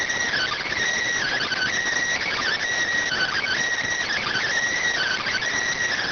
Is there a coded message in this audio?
Yes: Emergency Alert System (SAME) tones (header bursts) from 1.2 to 5.5 s, heard as harsh buzzing bursts.